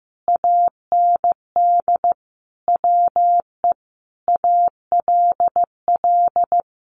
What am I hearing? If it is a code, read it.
Morse code, heard as a keyed tone, about 15 wpm, 700 Hz. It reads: AND WE ALL
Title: Sentences from Top 1100 Words 15wpm